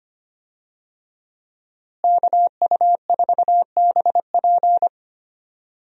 Morse code, heard as a keyed tone, about 25 wpm, 700 Hz. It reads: KU4BP
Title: USA Call Signs - 5 Characters - Repeated 2x - 25wpm